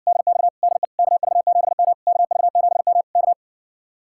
Morse code, heard as a keyed tone, about 60 wpm, 700 Hz. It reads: DX DE KV7M KV7M K